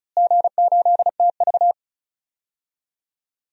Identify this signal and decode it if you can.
Morse code, heard as a keyed tone, about 36 wpm, 700 Hz. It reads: G8TV
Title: DXCC-100 Call Signs - 4 Characters - Repeated 3x - 35wpm